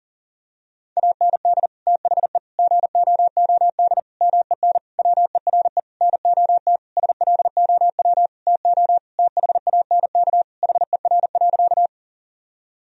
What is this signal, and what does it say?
Morse code, heard as a keyed tone, about 37 wpm, 700 Hz. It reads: AND THE GOOD MEN WERE NOT SLOW TO THANK HER.
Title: Sentences from Top 1000 Words - Review Part 2 - 40wpm